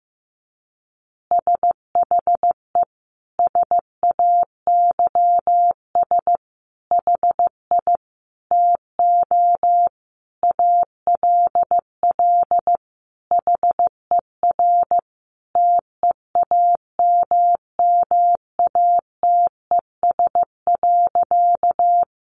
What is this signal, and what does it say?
Morse code, heard as a keyed tone, about 15 wpm, 700 Hz. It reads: SHE SAYS HI TO ALL HER TEAMMATES.
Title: Easy Phrases 15wpm